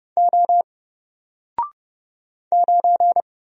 Morse code, heard as a keyed tone, about 30 wpm, 700 Hz. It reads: O 9
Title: Single Character (Letter:Number:Punctuation:Prosign) Alt 30wpm